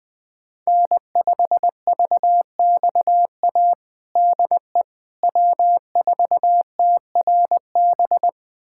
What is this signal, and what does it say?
Morse code, heard as a keyed tone, about 20 wpm, 700 Hz. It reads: N5VXA DE W4TRB
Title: USA Callsign DE Callsign - 5 Characters - 20wpm